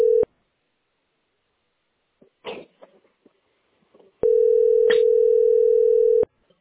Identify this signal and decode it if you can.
Ringback tone (North America), 440 + 480 Hz.